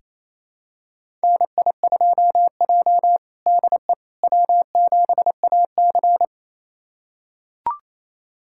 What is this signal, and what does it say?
Morse code, heard as a keyed tone, about 27 wpm, 700 Hz. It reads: NI2J DE W7AC
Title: USA Callsign DE Callsign - 4 Characters - 28wpm